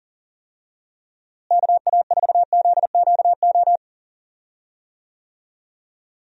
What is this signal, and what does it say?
Morse code, heard as a keyed tone, about 38 wpm, 700 Hz. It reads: KA4ZQO
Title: USA Call Signs - 6 Characters - Repeated 3x - 40wpm